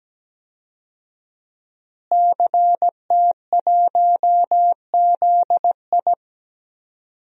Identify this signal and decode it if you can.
Morse code, heard as a keyed tone, about 17 wpm, 700 Hz. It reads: CT1ZI
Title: DXCC-100 Call Signs - 5 Characters - Repeated 2x - 17wpm